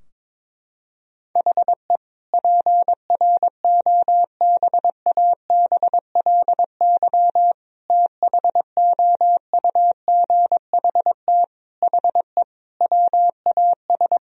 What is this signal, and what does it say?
Morse code, heard as a keyed tone, about 22 wpm, 700 Hz. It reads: HE PROBABLY THOUGHT HE WAS
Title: Sentences from Top 1000 Words - Review Part 4 - 22wpm